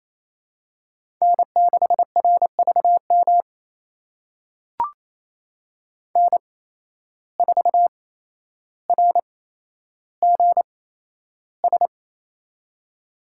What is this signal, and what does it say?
Morse code, heard as a keyed tone, about 27 wpm, spaced as for 12 wpm, 700 Hz. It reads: N6RVM N4RGS
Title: USA Call Signs - 5 Characters - 8x Character Spacing - 28wpm